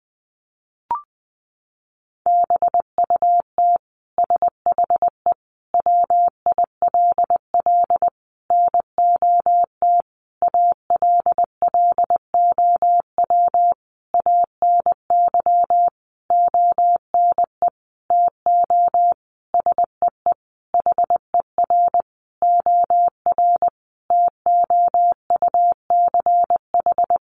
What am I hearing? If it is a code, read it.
Morse code, heard as a keyed tone, about 20 wpm, 700 Hz. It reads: BUT SHE WILL NOT ALLOW ANY ONE TO SEE HER OR TOUCH
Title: Sentences from Top 1200 Words 20wpm